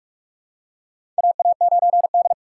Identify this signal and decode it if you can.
Morse code, heard as a keyed tone, about 44 wpm, 690 Hz. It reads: AA9D
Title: USA Call Signs - 4 Characters - 2x Character Spacing - 45wpm